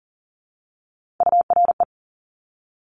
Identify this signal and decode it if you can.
Morse code, heard as a keyed tone, about 40 wpm, 700 Hz. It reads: URE
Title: Top Three Letter Combinations in Words 40wpm